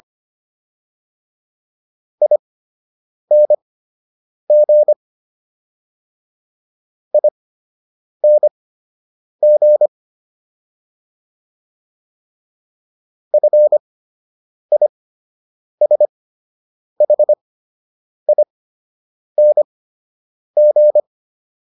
Morse code, heard as a keyed tone, about 25 wpm, spaced as for 8 wpm, 600 Hz. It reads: ING ING FISHING